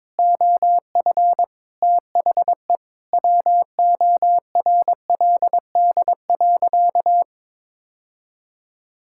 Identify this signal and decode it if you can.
Morse code, heard as a keyed tone, about 22 wpm, 700 Hz. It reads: OF THE WORLD.
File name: practice / 22wpm / Sentences with 3-Word Phrases - 22wpm